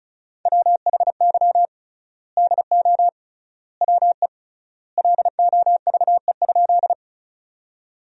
Morse code, heard as a keyed tone, about 36 wpm, 700 Hz. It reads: WHY DO WE LOVE?
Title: 3x Word-Spacing - Sentences from Top 400 Words - 35wpm